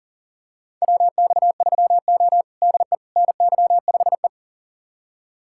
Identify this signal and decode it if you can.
Morse code, heard as a keyed tone, about 40 wpm, 700 Hz. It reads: WX3O DE NY5E
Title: USA Callsign DE Callsign - 4 Characters - 40wpm